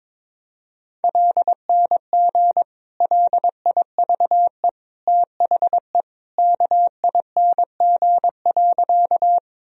Morse code, heard as a keyed tone, about 22 wpm, 700 Hz. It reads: LNG LIVE THE KING.
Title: Sentences from Top 400 Words and Abbreviations 22wpm